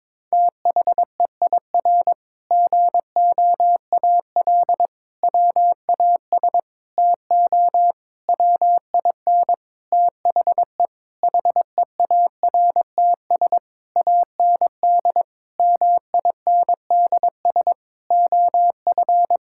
Morse code, heard as a keyed tone, about 21 wpm, 700 Hz. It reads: THEIR GOAL WAS TO WIN THE HEARTS AND MINDS OF
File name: practice / 22wpm / Sentences with 8-Word Phrases - 22wpm